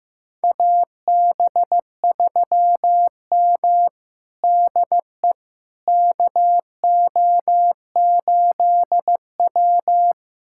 Morse code, heard as a keyed tone, about 15 wpm, 700 Hz. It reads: AB3M DE KO8W
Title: USA Callsign DE Callsign - 4 Characters - 15wpm